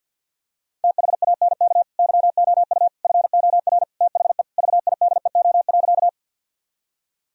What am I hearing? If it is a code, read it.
Morse code, heard as a keyed tone, about 50 wpm, 700 Hz. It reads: THANK YOU FOR THE VIDEO.
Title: Sentences from Top 1000 Words - Review Part 4 - 50wpm